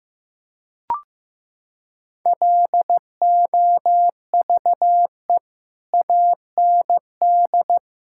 Morse code, heard as a keyed tone, about 15 wpm, 700 Hz. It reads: LOVE AND
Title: Sentences from Top 1000 Words - Review Part 5 - 15wpm